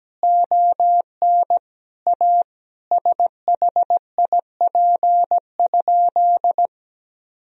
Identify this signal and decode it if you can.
Morse code, heard as a keyed tone, about 17 wpm, 700 Hz. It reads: ON A SHIP?
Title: Sentences from Top 1000 Words - Review Part 2 - 17wpm